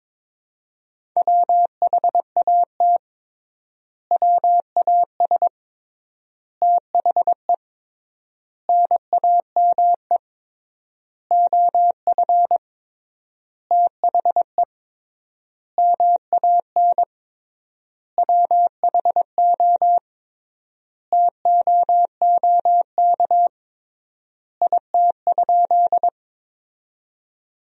Morse code, heard as a keyed tone, about 22 wpm, 700 Hz. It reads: WHAT WAS THE NAME OF THE MAN WHO TOOK IT?
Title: 3x Word-Spacing - Sentences from Top 300 Words - 22wpm